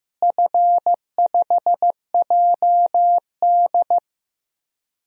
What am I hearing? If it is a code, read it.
Morse code, heard as a keyed tone, about 15 wpm, 700 Hz. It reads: F5JD
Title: DXCC-100 Call Signs - 4 Characters - Repeated 3x - 15wpm